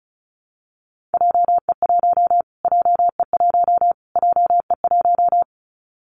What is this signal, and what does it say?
Morse code, heard as a keyed tone, about 34 wpm, 700 Hz. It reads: JE1 JE1 JE1
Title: Callsign Prefixes for Top 25 DXCC - Repeated 3x - 35wpm